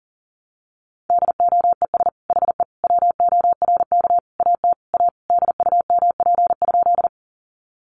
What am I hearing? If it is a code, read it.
Morse code, heard as a keyed tone, about 40 wpm, 700 Hz. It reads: DOES HE WORK AT A DUMP?
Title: Sentences from Top 1300 Words 40wpm